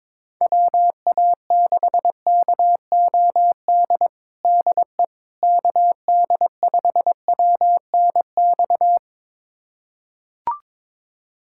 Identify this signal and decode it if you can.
Morse code, heard as a keyed tone, about 22 wpm, 700 Hz. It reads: WA6KOD DE KD5WNX
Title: USA Callsign DE Callsign - 6 Characters - 22wpm